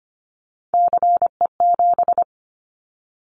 Morse code, heard as a keyed tone, about 25 wpm, 700 Hz. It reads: CE7
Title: Callsign Prefixes for Top 100 DXCC - 25wpm